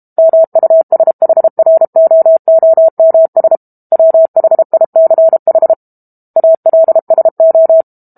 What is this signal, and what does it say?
Morse code, heard as a keyed tone, about 32 wpm, 650 Hz. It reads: MUSHROOMS WHICH ALSO
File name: Book Two / 06-Chapter 6 - 32 WPM